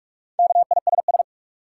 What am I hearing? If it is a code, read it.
Morse code, heard as a keyed tone, about 42 wpm, 700 Hz. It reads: KISS
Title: Four Letter Words - 45wpm